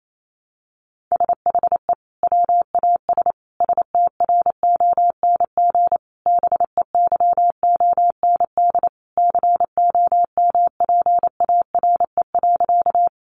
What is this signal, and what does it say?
Morse code, heard as a keyed tone, about 28 wpm, 700 Hz. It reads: SHE WAS STRONG BEYOND COMPARE.
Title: Sentences from Top 1300 Words 28wpm